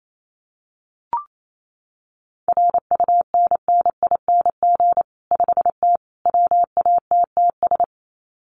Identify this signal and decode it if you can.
Morse code, heard as a keyed tone, about 28 wpm, 700 Hz. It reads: RUNNING 5T WATTS
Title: Common QSO Phrases 28wpm